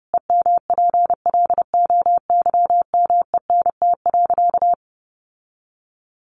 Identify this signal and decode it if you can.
Morse code, heard as a keyed tone, about 30 wpm, 700 Hz. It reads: EMPLOYMENT.
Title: Sentences from Top 1500 Words 30wpm